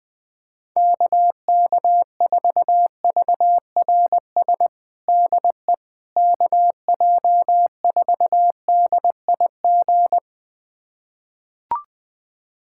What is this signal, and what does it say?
Morse code, heard as a keyed tone, about 20 wpm, 700 Hz. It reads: KK4VRS DE KJ4DIG
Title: USA Callsign DE Callsign - 6 Characters - 20wpm